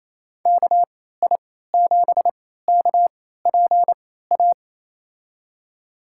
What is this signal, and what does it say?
Morse code, heard as a keyed tone, about 28 wpm, spaced as for 16 wpm, 700 Hz. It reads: KI7KPA
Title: USA Call Signs - 6 Characters - 3x Character Spacing - 28wpm